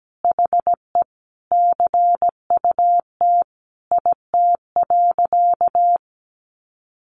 Morse code, heard as a keyed tone, about 17 wpm, 700 Hz. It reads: HE CUT IT.
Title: Sentences from Top 1000 Words 17wpm